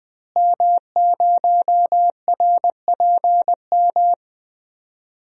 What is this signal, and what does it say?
Morse code, heard as a keyed tone, about 20 wpm, 700 Hz. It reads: M0RPM
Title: DXCC-100 Call Signs - 5 Characters - Repeated 3x - 20wpm